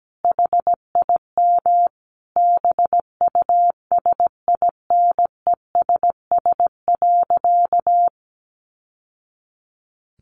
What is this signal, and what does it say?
Morse code, heard as a keyed tone, about 17 wpm, 700 Hz. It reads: HIM BUSINESS.
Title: Sentences from Top 600 Words 17wpm